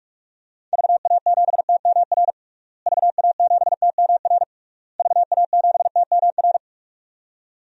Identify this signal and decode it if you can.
Morse code, heard as a keyed tone, about 41 wpm, 700 Hz. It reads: VA7TMR VA7TMR VA7TMR
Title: DXCC-100 Call Signs - 6 Characters - Repeated 3x - 45wpm